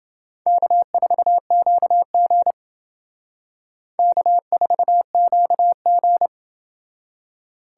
Morse code, heard as a keyed tone, about 30 wpm, 700 Hz. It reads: K4QG K4QG